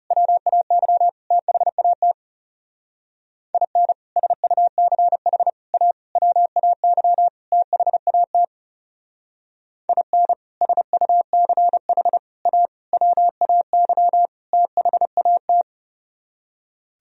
Morse code, heard as a keyed tone, about 36 wpm, 700 Hz. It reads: WAY THAT IN SUCH A WAY THAT IN SUCH A WAY THAT
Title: Speed-Racing - Sets of 5 Words 30wpm